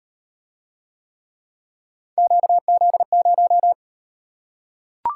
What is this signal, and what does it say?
Morse code, heard as a keyed tone, about 38 wpm, 700 Hz. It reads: QZ0